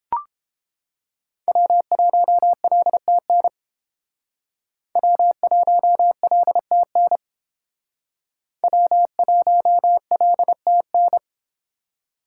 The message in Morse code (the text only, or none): W1LTN W1LTN W1LTN